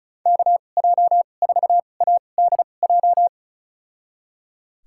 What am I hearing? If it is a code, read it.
Morse code, heard as a keyed tone, about 34 wpm, spaced as for 25 wpm, 700 Hz. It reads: KJ4ADJ
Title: USA Call Signs - 6 Characters - 2x Character Spacing - 35wpm